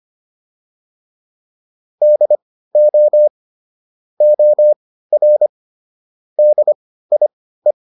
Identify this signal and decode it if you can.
Morse code, heard as a keyed tone, about 25 wpm, spaced as for 15 wpm, 600 Hz. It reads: DO OR DIE